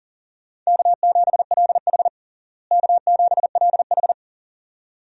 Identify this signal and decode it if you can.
Morse code, heard as a keyed tone, about 40 wpm, 700 Hz. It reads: K7LH K7LH
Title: USA Call Signs - 4 Characters - Repeated 2x - 40wpm